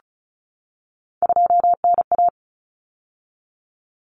Morse code, heard as a keyed tone, about 34 wpm, 700 Hz. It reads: 2NA
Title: Common Cut Numbers - 3-Digit 35wpm